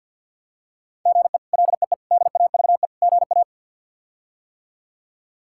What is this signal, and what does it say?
Morse code, heard as a keyed tone, about 50 wpm, 700 Hz. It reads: GE LEE DAVE GA
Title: K1USN SST QSO Practice - 50wpm